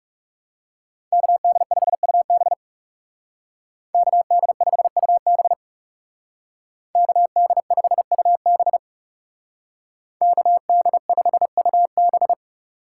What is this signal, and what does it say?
Morse code, heard as a keyed tone, about 39 wpm, 700 Hz. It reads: KD5UB KD5UB KD5UB KD5UB